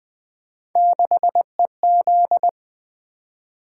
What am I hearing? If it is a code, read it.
Morse code, heard as a keyed tone, about 20 wpm, 700 Hz. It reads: 6EZ
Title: Three Letter-Number Combinations - 20wpm